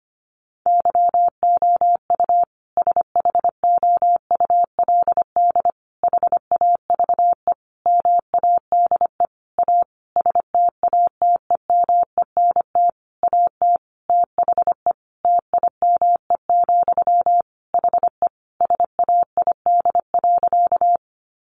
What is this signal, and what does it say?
Morse code, heard as a keyed tone, about 25 wpm, 700 Hz. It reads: YOU SHOULD HAVE MADE A STATEMENT AT THE TIME, HE SAID.
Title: Sentences from Top 1200 Words 25wpm